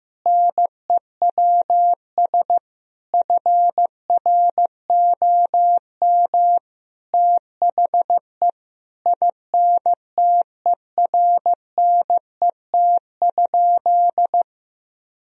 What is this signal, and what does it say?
Morse code, heard as a keyed tone, about 15 wpm, 700 Hz. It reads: NEWS FROM THE INTERNET?
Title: Sentences from Top 1000 Words - Review Part 4 - 15wpm